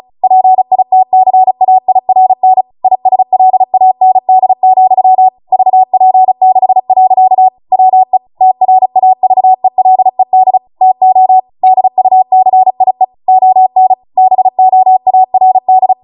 Morse code, heard as a keyed tone, about 36 wpm, 750 Hz. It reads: PITCAIRN ISLAND, VP6. WE TRAVELED TO DUCIE ON BOARD